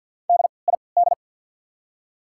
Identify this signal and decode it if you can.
Morse code, heard as a keyed tone, about 53 wpm, spaced as for 29 wpm, 690 Hz. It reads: DID
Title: 3x Character Spacing - Top 100 Words - 50wpm